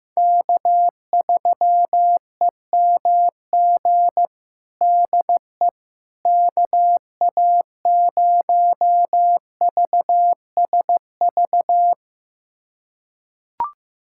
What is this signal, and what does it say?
Morse code, heard as a keyed tone, about 15 wpm, 700 Hz. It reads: K3EMG DE KA0VSV
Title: USA Callsign DE Callsign 15wpm